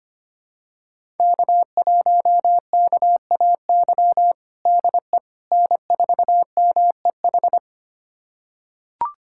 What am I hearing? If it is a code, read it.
Morse code, heard as a keyed tone, about 25 wpm, 700 Hz. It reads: K1KAY DE N4MEH
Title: USA Callsign DE Callsign - 5 Characters - 25wpm